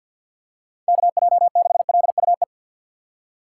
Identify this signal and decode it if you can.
Morse code, heard as a keyed tone, about 46 wpm, 700 Hz. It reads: KJ6LUE
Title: USA Call Signs - 2x Character Spacing - 50wpm